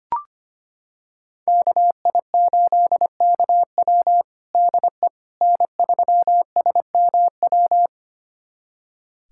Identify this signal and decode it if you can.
Morse code, heard as a keyed tone, about 25 wpm, 700 Hz. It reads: KI8KW DE N3SMW
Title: USA Callsign DE Callsign - 5 Characters - 25wpm